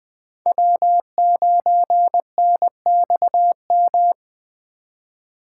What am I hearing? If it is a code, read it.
Morse code, heard as a keyed tone, about 24 wpm, 700 Hz. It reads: W9NXM
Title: Speed Racing - USA Call Signs 20wpm